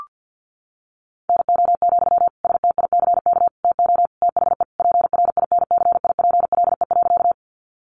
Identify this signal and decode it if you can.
Morse code, heard as a keyed tone, about 50 wpm, 700 Hz. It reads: NO, STICK TO THE PRINCIPLE.
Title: Sentences from Top 1400 Words 50wpm